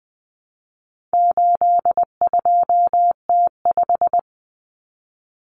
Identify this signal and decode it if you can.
Morse code, heard as a keyed tone, about 20 wpm, 700 Hz. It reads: 82T5
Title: Common Cut Numbers - 4-Digit 20wpm